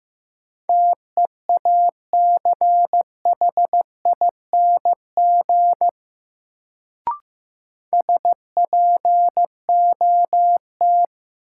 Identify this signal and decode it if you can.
Morse code, heard as a keyed tone, about 15 wpm, 700 Hz. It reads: TEACHING SPOT
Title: Sentences from Top 1100 Words - New Word Drill - 15wpm